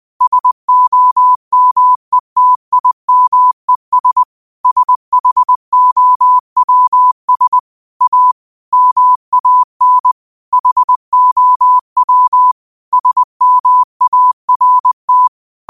Morse code, heard as a keyed tone, about 20 wpm, 1000 Hz. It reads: SOMETIMES SHOWS A MAN HOW SMART
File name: qotdfiles / 20wpm